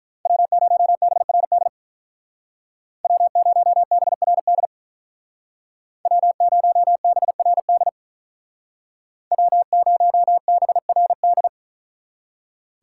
Morse code, heard as a keyed tone, about 43 wpm, 700 Hz. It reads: W0BRD W0BRD W0BRD W0BRD
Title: Speed Racing - USA 1x3 Call Signs 35wpm